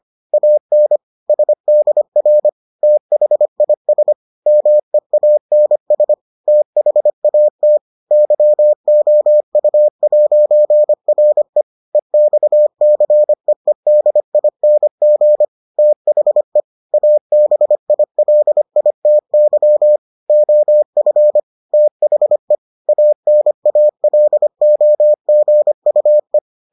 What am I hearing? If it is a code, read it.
Morse code, heard as a keyed tone, about 25 wpm, 600 Hz. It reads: AN SDR THIS MEANS THAT YOU'RE EXCEEDING THE ABILITY OF THE ANALOGUE